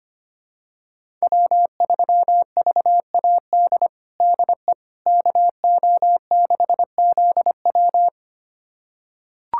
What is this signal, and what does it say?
Morse code, heard as a keyed tone, about 25 wpm, 700 Hz. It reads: W3VAD DE KO6ZW
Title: USA Callsign DE Callsign - 5 Characters - 25wpm